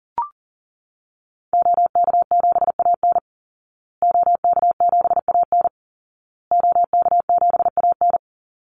Morse code, heard as a keyed tone, about 40 wpm, 700 Hz. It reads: OK7AN OK7AN OK7AN
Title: Call Signs Repeated 2x - 40wpm